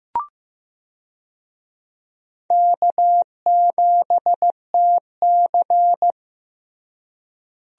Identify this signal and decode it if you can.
Morse code, heard as a keyed tone, about 15 wpm, 700 Hz. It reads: K7TC
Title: USA Call Signs - 4 Characters - Repeated 2x - 15wpm